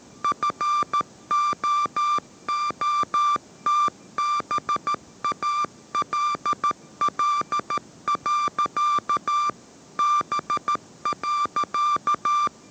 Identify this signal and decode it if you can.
Morse code, heard as a keyed tone, about 13 wpm, 1180 Hz. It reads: FOOTBALL. B.